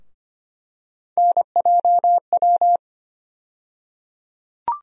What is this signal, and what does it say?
Morse code, heard as a keyed tone, about 25 wpm, 700 Hz. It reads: NJW
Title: USA Call Sign Postfixes - 3 Letters 25wpm